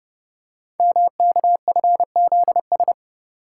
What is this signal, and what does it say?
Morse code, heard as a keyed tone, about 30 wpm, 700 Hz. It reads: MKFZS
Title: Code Groups - 5 Letters - 30wpm